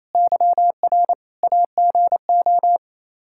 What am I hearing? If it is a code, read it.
Morse code, heard as a keyed tone, about 28 wpm, 700 Hz. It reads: YR AGO
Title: Practice QSO - Exchange 3 - More Info - 28wpm